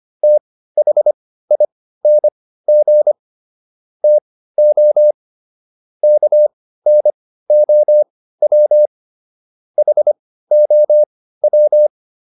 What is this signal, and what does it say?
Morse code, heard as a keyed tone, about 25 wpm, spaced as for 15 wpm, 600 Hz. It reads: THING TO KNOW HOW